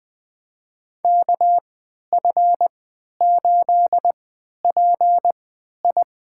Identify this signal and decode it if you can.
Morse code, heard as a keyed tone, about 20 wpm, spaced as for 11 wpm, 700 Hz. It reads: KF8PI
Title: USA Call Signs - 5 Characters - 3x Character Spacing - 20wpm